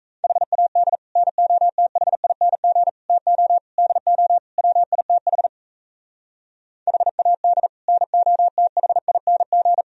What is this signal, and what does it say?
Morse code, heard as a keyed tone, about 37 wpm, 700 Hz. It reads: HAD NOTHING TO DO WITH HAD NOTHING